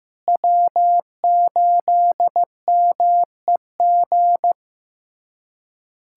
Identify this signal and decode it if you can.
Morse code, heard as a keyed tone, about 16 wpm, 700 Hz. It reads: W8MEG
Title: USA Callsign DE Callsign - 5 Characters - 15wpm